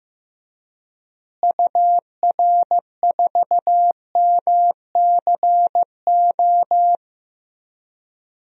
Morse code, heard as a keyed tone, about 15 wpm, 700 Hz. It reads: UR4MCO
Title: DXCC-100 Call Signs - 6 Characters - Repeated 2x - 15wpm